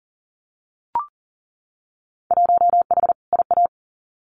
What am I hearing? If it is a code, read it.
Morse code, heard as a keyed tone, about 39 wpm, 700 Hz. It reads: 1H IA
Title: Winter Field Day Exchanges 40wpm